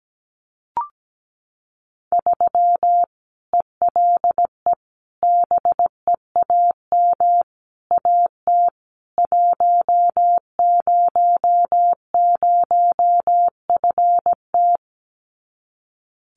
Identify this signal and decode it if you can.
Morse code, heard as a keyed tone, about 17 wpm, 700 Hz. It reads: 3 ELE BEAM AT 100FT
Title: Common QSO Phrases 17wpm